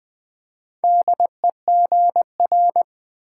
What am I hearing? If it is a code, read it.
Morse code, heard as a keyed tone, about 20 wpm, 700 Hz. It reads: DEGR
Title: Build A Word - 2 Letters At A Time - 6-Letter Words - 20wpm